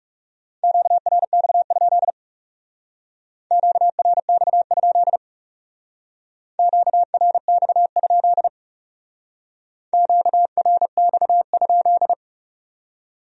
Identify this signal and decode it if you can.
Morse code, heard as a keyed tone, about 35 wpm, 700 Hz. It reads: QRX? QRX? QRX? QRX?
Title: Speed-Racing Q-Codes - 30wpm